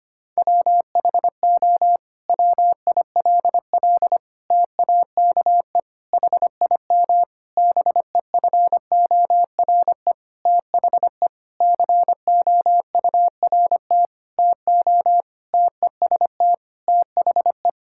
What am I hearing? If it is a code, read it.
Morse code, heard as a keyed tone, about 25 wpm, 700 Hz. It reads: WHO WILL TAKE HIM BEFORE THE COURT TO TEST THE
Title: Sentences from Top 1600 Words 25wpm